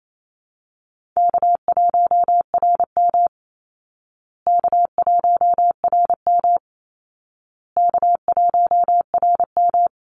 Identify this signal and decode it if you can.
Morse code, heard as a keyed tone, about 28 wpm, 700 Hz. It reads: K1RM K1RM K1RM
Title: Easy Call Signs Repeated 3x - 28wpm